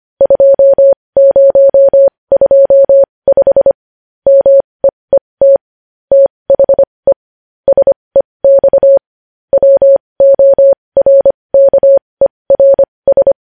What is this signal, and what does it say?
Morse code, heard as a keyed tone, about 25 wpm, spaced as for 20 wpm, 550 Hz. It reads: 2025 MEET THE SEX WORKERS
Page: News Headlines in Morse Code at 20 WPM